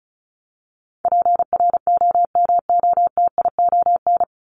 Morse code, heard as a keyed tone, about 34 wpm, 700 Hz. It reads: PROMOTION
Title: Top 1501-1750 Words 35wpm